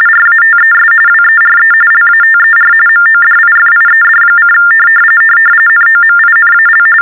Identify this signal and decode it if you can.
TTY/TDD text telephone tones, about 45.45 baud. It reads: RATES WERE LATER MODIFIED: NOMINAL
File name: teletype-RTTY-45